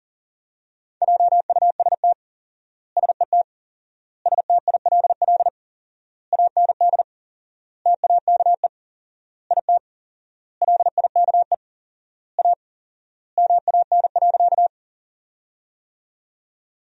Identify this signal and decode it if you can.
Morse code, heard as a keyed tone, about 40 wpm, 700 Hz. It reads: JUST SET STILL AND TAKE IT LIKE A MAN.
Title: 4x Word-Spacing - Sentences from Top 300 Words - 40wpm